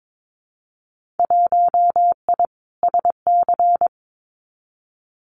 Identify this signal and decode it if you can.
Morse code, heard as a keyed tone, about 22 wpm, 700 Hz. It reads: 1I SC
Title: Winter Field Day Exchanges 22wpm